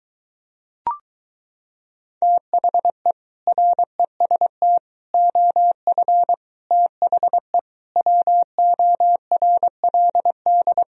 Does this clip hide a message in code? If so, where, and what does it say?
Yes: Morse code from 2.2 to 10.8 s, heard as a keyed tone, about 23 wpm, 700 Hz. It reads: THE REST OF THE WORLD